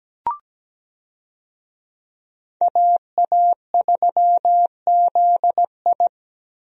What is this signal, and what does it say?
Morse code, heard as a keyed tone, about 17 wpm, 700 Hz. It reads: AA3ZI
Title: USA Call Signs - 5 Characters - Repeated 3x - 17wpm